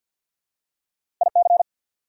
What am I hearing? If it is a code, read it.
Morse code, heard as a keyed tone, about 50 wpm, 700 Hz. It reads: IC
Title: Prefixes and Suffixes 50wpm